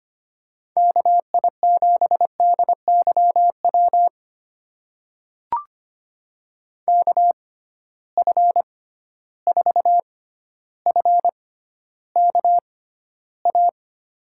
Morse code, heard as a keyed tone, about 25 wpm, spaced as for 13 wpm, 700 Hz. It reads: KI7DYW KF4FKA
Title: USA Call Signs - 6 Characters - 6x Character Spacing - 25wpm